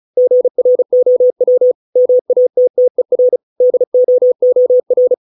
Morse code, heard as a keyed tone, about 34 wpm, 500 Hz. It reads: GROW MATTER DOOR
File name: 632 words / 632 Words 35WPM